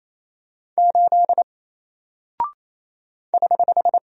Morse code, heard as a keyed tone, about 27 wpm, 700 Hz. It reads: 8 <HH>
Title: Single Character (Letter:Number:Punctuation:Prosign) 28wpm